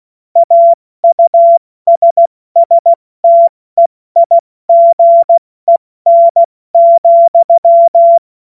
Morse code, heard as a keyed tone, about 16 wpm, 680 Hz. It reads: AUSSTEIGEN,
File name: Morsekurs / Witze / Witz 31. August 2021, 70 F80